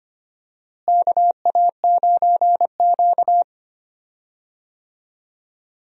Morse code, heard as a keyed tone, about 25 wpm, 700 Hz. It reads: KA9Q
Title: USA Call Signs - 4 Characters - Repeated 2x - 25wpm